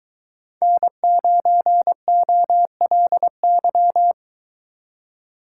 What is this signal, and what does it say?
Morse code, heard as a keyed tone, about 23 wpm, 700 Hz. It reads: N9OLY